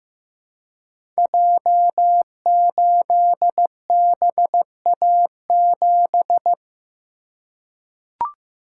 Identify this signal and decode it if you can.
Morse code, heard as a keyed tone, about 15 wpm, 700 Hz. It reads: J8BA7
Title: Code Groups - 5 Letter:Numbers - 15wpm